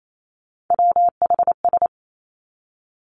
Morse code, heard as a keyed tone, about 28 wpm, 700 Hz. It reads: WHS
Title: Three Challenging Letter-Number Combinations 28wpm